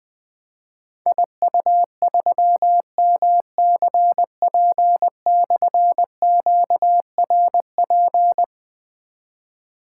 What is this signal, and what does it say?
Morse code, heard as a keyed tone, about 20 wpm, 700 Hz. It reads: IU3MCP/QRP
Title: Difficult Call Signs 20wpm